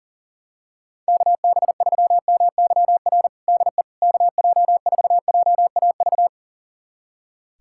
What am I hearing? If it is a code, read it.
Morse code, heard as a keyed tone, about 40 wpm, 700 Hz. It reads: KB3MYR DE KJ4JAV
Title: USA Callsign DE Callsign - 6 Characters - 40wpm